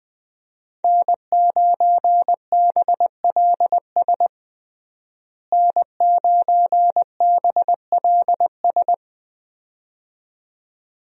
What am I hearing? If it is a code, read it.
Morse code, heard as a keyed tone, about 20 wpm, 700 Hz. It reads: N9BLS N9BLS
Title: USA Call Signs - 5 Characters - Repeated 2x - 20wpm